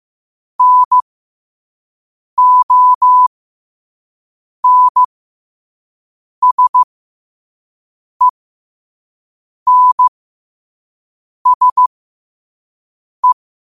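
Morse code, heard as a keyed tone, about 15 wpm, spaced as for 5 wpm, 1000 Hz. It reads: NONSENSE
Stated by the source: Morse code sent at 5 wpm.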